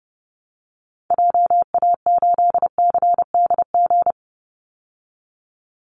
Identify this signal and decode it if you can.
Morse code, heard as a keyed tone, about 30 wpm, 700 Hz. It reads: JA8CDG
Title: Call Signs Repeated 3x - 30wpm